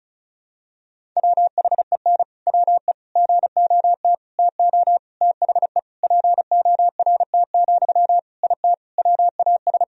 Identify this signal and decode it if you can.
Morse code, heard as a keyed tone, about 34 wpm, 700 Hz. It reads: WHEN WE GOT TO THE PORT, IT WAS
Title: Sentences from Top 1000 Words - Review Part 1 - 35wpm